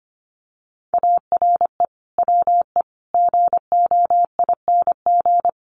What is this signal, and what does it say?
Morse code, heard as a keyed tone, about 25 wpm, 700 Hz. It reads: ARE WE GOING
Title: Sets of 5 Words 25wpm